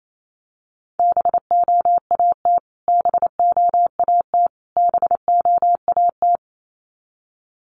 Morse code, heard as a keyed tone, about 28 wpm, 700 Hz. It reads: BOAT BOAT BOAT
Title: Top 500 Words Repeated 28wpm